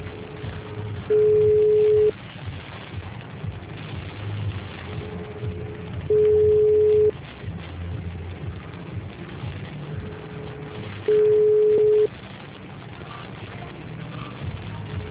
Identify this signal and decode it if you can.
Ringback tone (Europe, 425 Hz).